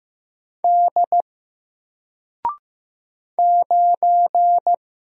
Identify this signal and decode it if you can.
Morse code, heard as a keyed tone, about 15 wpm, 700 Hz. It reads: D 9
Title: Single Character (Letter:Number:Punctuation:Prosign) Alt 15wpm